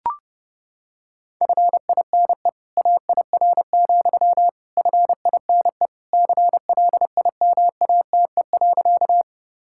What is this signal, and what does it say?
Morse code, heard as a keyed tone, about 30 wpm, 700 Hz. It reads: FINE AIR, FINE CLIMATE.
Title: Sentences from Top 1600 Words 30wpm